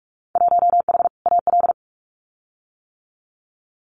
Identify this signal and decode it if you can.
Morse code, heard as a keyed tone, about 44 wpm, 700 Hz. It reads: 1H AL
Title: Winter Field Day Exchanges 45wpm